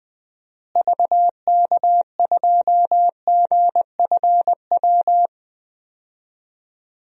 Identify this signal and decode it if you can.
Morse code, heard as a keyed tone, about 20 wpm, 700 Hz. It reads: VK2GFW
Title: DXCC-100 Call Signs - 6 Characters - Repeated 2x - 20wpm